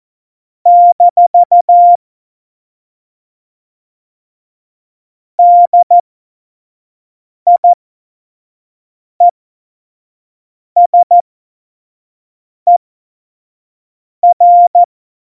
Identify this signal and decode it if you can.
Morse code, heard as a keyed tone, about 14 wpm, spaced as for 5 wpm, 700 Hz. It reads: - DIESER
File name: Morsekurs / Witze / Witz 22. Januar 2019, 25 F70